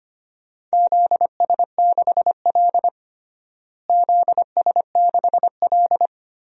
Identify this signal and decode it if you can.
Morse code, heard as a keyed tone, about 25 wpm, 700 Hz. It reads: ZS6L ZS6L
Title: DXCC-100 Call Signs - 4 Characters - Repeated 2x - 25wpm